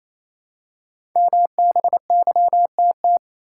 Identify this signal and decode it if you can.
Morse code, heard as a keyed tone, about 27 wpm, 700 Hz. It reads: MBYTT
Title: Code Groups - 5 Letter:Numbers - 28wpm